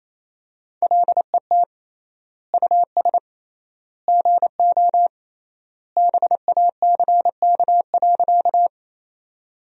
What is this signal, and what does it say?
Morse code, heard as a keyed tone, about 28 wpm, 700 Hz. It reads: LET US GO BACK.
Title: 3x Word-Spacing - Sentences from Top 300 Words - 28wpm